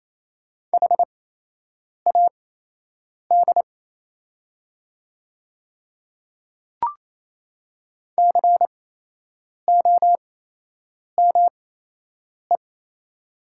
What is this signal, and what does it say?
Morse code, heard as a keyed tone, about 27 wpm, spaced as for 8 wpm, 700 Hz. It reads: HAD COME
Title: 8x Character Spacing - Top 300 Words - 28wpm